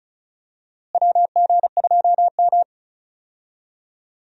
Morse code, heard as a keyed tone, about 34 wpm, 700 Hz. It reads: WG2M
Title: USA Call Signs - 4 Characters - Repeated 2x - 35wpm